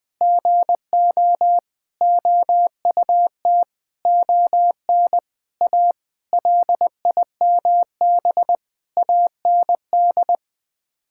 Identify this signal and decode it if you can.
Morse code, heard as a keyed tone, about 20 wpm, 700 Hz. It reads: GO OUT ON A LIMB AND
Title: Sets of 8 Words - 20wpm